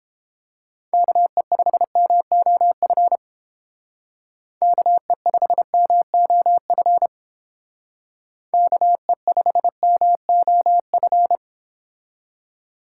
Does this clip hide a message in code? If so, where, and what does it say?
Yes: Morse code from 0.9 to 11.4 s, heard as a keyed tone, about 29 wpm, 690 Hz. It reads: KE5MOF KE5MOF KE5MOF